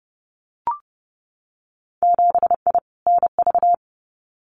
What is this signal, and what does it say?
Morse code, heard as a keyed tone, about 30 wpm, 700 Hz. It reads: 7I NV
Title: Winter Field Day Exchanges Encore 30wpm